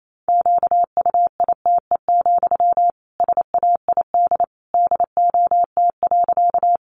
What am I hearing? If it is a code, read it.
Morse code, heard as a keyed tone, about 28 wpm, 700 Hz. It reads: QUITE, SAID DOT.
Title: Sentences from Top 1500 Words 28wpm